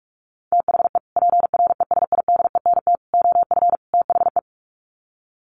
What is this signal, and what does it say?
Morse code, heard as a keyed tone, about 44 wpm, 700 Hz. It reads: THE PRESIDENT OF THE
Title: Sets of 4 Words 45wpm